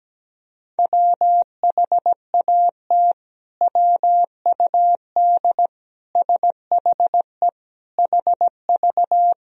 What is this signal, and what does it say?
Morse code, heard as a keyed tone, about 17 wpm, 700 Hz. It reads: WHAT WUD SHE HV
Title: Sentences from Top 100 Words and Abbreviations 17wpm